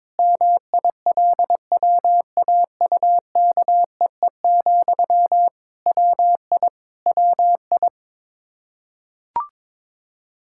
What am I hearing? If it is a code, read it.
Morse code, heard as a keyed tone, about 22 wpm, 700 Hz. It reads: MILWAUKEE, WI WI
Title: Practice QSO - Exchange 1 - Beginning - 22wpm